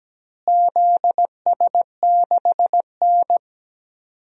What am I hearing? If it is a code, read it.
Morse code, heard as a keyed tone, about 17 wpm, 700 Hz. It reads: ZS6N
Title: DXCC-100 Call Signs - 4 Characters - Repeated 3x - 17wpm